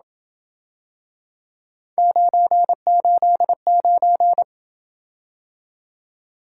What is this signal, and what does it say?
Morse code, heard as a keyed tone, about 27 wpm, 700 Hz. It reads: 989